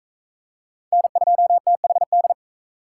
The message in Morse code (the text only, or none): N2THD